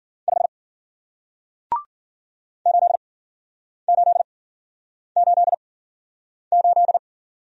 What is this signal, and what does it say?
Morse code, heard as a keyed tone, about 50 wpm, 700 Hz. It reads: 5 8 8 8 8
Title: Speed-Racing Single Letter-Number 40wpm